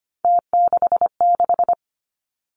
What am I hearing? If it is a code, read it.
Morse code, heard as a keyed tone, about 25 wpm, 700 Hz. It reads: T66
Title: Three Challenging Letter-Number Combinations 25wpm